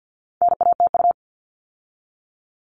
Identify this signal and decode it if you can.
Morse code, heard as a keyed tone, about 50 wpm, 700 Hz. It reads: NATU
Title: All Cut Numbers - 4-Digit 50wpm